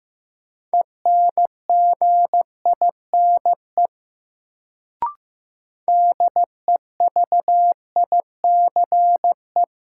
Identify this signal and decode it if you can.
Morse code, heard as a keyed tone, about 15 wpm, 700 Hz. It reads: ENGINE DEVICE
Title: Sentences from Top 1500 Words - New Word Drill Alt - 15wpm